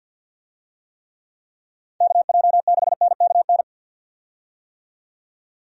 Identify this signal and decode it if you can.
Morse code, heard as a keyed tone, about 53 wpm, 700 Hz. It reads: KJ6NKN
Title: USA Call Signs - 6 Characters - 50wpm